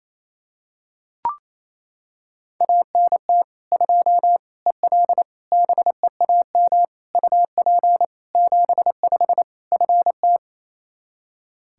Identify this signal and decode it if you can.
Morse code, heard as a keyed tone, about 28 wpm, 700 Hz. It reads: ANT 2 EL BEAM UP 75 FT
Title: Practice QSO - Exchange 2 - Info - 28wpm